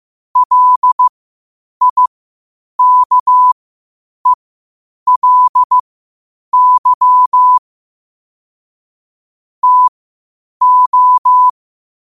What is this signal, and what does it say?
Morse code, heard as a keyed tone, about 15 wpm, spaced as for 8 wpm, 1000 Hz. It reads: LIKELY TO
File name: qotdfiles / 8wpm